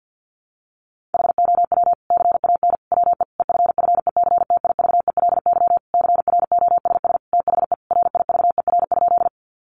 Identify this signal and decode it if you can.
Morse code, heard as a keyed tone, about 50 wpm, 700 Hz. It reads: HOW CAN WE EFFECTIVELY CROSS THE RIVER?
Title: Sentences from Top 1400 Words 50wpm